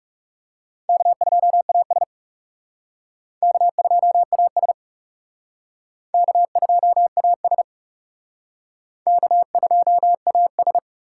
Morse code, heard as a keyed tone, about 35 wpm, 700 Hz. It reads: K2AS K2AS K2AS K2AS
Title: Speed Racing - USA 1x2 Call Signs 30wpm